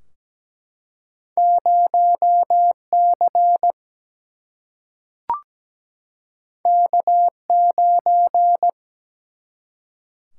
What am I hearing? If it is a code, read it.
Morse code, heard as a keyed tone, about 17 wpm, 700 Hz. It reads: 0C K9
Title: Single-Letter Single-Number 17wpm